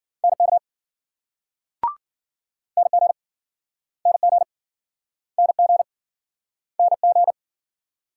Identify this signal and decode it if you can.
Morse code, heard as a keyed tone, about 52 wpm, 700 Hz. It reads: NK NG NG NG NG